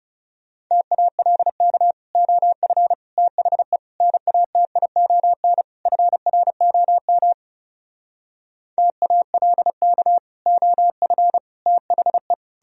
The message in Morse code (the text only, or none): TALK OF THE NATION FROM TALK OF THE